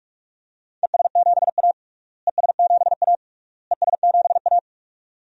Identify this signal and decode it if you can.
Morse code, heard as a keyed tone, about 41 wpm, 700 Hz. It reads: ES7A ES7A ES7A
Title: DXCC-100 Call Signs - 4 Characters - Repeated 3x - 45wpm